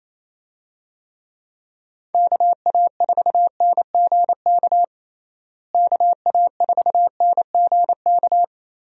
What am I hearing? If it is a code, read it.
Morse code, heard as a keyed tone, about 28 wpm, 700 Hz. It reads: KA4NGK KA4NGK
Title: USA Call Signs - 6 Characters - Repeated 3x - 28wpm